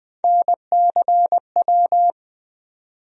Morse code, heard as a keyed tone, about 20 wpm, 700 Hz. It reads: NCW
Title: USA Call Sign Postfixes - 3 Letters 20wpm